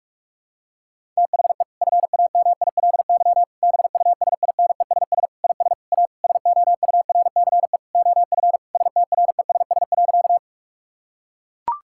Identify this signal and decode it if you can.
Morse code, heard as a keyed tone, about 41 wpm, 700 Hz. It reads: THE FAMILY BUSINESS IS A SOURCE OF STRESS.
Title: Sentences from Top 1000 Words - Review Part 6 - 45wpm